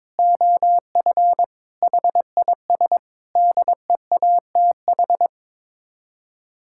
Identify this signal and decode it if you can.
Morse code, heard as a keyed tone, about 22 wpm, 700 Hz. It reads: OF HIS DEATH
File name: practice / 22wpm / Sets of 6 Words - 22wpm